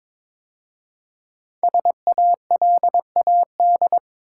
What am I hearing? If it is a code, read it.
Morse code, heard as a keyed tone, about 22 wpm, 700 Hz. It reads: SALAD
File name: 5 Letter Words - First 3 Letters Given - 22wpm